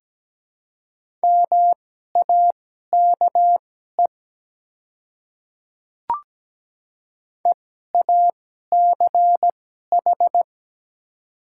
Morse code, heard as a keyed tone, about 17 wpm, spaced as for 12 wpm, 700 Hz. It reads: MAKE EACH
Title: 2x Character Spacing - Top 300 Words - 17wpm